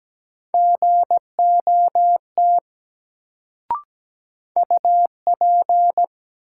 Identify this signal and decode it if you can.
Morse code, heard as a keyed tone, about 17 wpm, 700 Hz. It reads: GOT UP
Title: Sets of 2 Words - New Word Drill - 17wpm